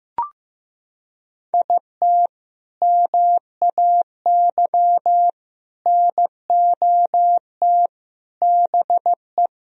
Morse code, heard as a keyed tone, about 15 wpm, 700 Hz. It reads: IT MAY NOT BE
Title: Sentences from Top 1000 Words - Review Part 4 - 15wpm